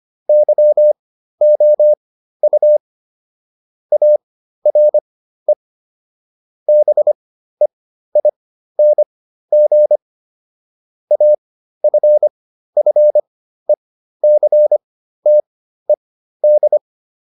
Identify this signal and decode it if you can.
Morse code, heard as a keyed tone, about 25 wpm, spaced as for 13 wpm, 600 Hz. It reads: YOU ARE BEING AFFECTED